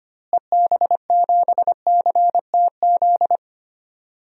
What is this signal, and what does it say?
Morse code, heard as a keyed tone, about 25 wpm, 700 Hz. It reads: EB7CTZ
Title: DXCC-100 Call Signs - 6 Characters - Repeated 3x - 25wpm